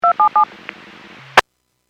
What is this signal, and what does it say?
DTMF, heard as telephone touch tones, tones of about 100 ms. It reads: 277
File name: WPS / 00042 06-39 WPS 451.25 WPS